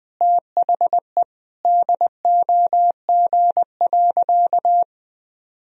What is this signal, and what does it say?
Morse code, heard as a keyed tone, about 21 wpm, 700 Hz. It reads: THE DOG.
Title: Sentences from Top 1000 Words - Review Part 2 - 20wpm